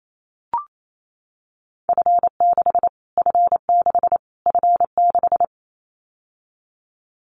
Morse code, heard as a keyed tone, about 28 wpm, 700 Hz. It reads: F6 F6 F6
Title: Callsign Prefixes for Top 25 DXCC - Repeated 3x - 28wpm